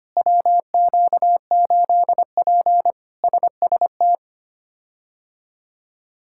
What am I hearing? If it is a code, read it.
Morse code, heard as a keyed tone, about 25 wpm, 700 Hz. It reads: WQ8P SST
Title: K1USN SST QSO Practice - 25wpm